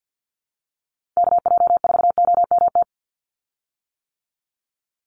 Morse code, heard as a keyed tone, about 50 wpm, 700 Hz. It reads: KJ4OMT
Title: USA Call Signs 50wpm